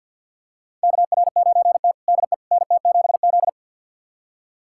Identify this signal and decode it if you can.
Morse code, heard as a keyed tone, about 50 wpm, 700 Hz. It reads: KR9T DE NT7Z
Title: USA Callsign DE Callsign - 4 Characters - 50wpm